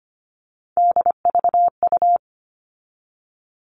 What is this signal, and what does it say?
Morse code, heard as a keyed tone, about 25 wpm, 700 Hz. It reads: DVU
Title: All Cut Numbers - 3-Digit 25wpm